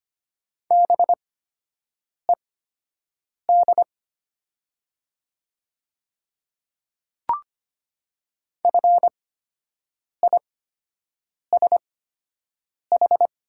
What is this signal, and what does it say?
Morse code, heard as a keyed tone, about 25 wpm, spaced as for 7 wpm, 700 Hz. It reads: BED FISH
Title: 8x Character Spacing - Top 500 Words - 25wpm